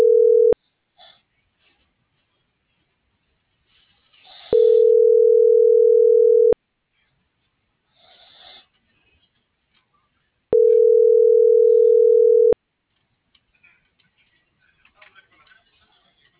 Ringback tone (North America), 440 + 480 Hz.